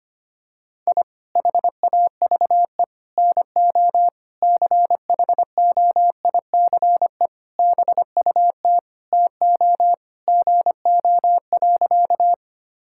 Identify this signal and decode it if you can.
Morse code, heard as a keyed tone, about 25 wpm, 700 Hz. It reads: I HAVE NO CHOICE BUT TO GO.
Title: Sentences from Top 1000 Words - Review Part 6 - 25wpm